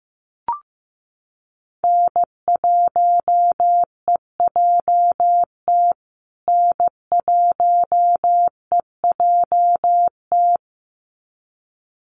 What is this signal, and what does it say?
Morse code, heard as a keyed tone, about 14 wpm, 700 Hz. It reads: N1EJT N1EJT
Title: USA Call Signs Repeated 2x 15wpm